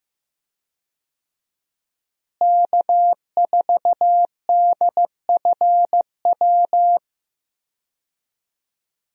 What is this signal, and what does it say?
Morse code, heard as a keyed tone, about 15 wpm, 700 Hz. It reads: K4DFW
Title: USA Call Signs - 5 Characters - Repeated 2x - 15wpm